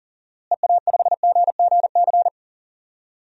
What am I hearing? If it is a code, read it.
Morse code, heard as a keyed tone, about 40 wpm, 700 Hz. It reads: EA5GGC
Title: DXCC-100 Call Signs - 6 Characters - Repeated 2x - 40wpm